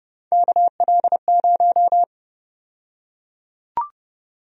Morse code, heard as a keyed tone, about 30 wpm, 700 Hz. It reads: KL0
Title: USA Call Sign Prefixes - 30wpm